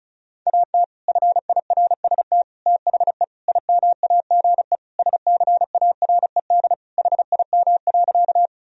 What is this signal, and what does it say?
Morse code, heard as a keyed tone, about 34 wpm, 700 Hz. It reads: AT FIRST THE IMAGE SCARED HIM.
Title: Sentences from Top 1000 Words - Review Part 4 - 35wpm